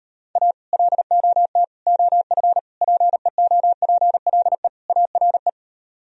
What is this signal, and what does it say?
Morse code, heard as a keyed tone, about 38 wpm, 700 Hz. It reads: A LOT OF PEOPLE ARE